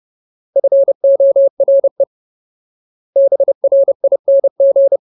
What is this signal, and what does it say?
Morse code, heard as a keyed tone, about 30 wpm, 550 Hz. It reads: FORE BRING
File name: Ultimate Word List / 30WPM Ultimate